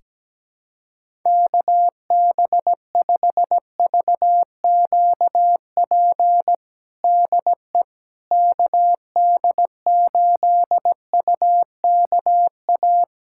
Morse code, heard as a keyed tone, about 17 wpm, 700 Hz. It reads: KB5VQP DE KD8UKA
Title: USA Callsign DE Callsign - 6 Characters - 17wpm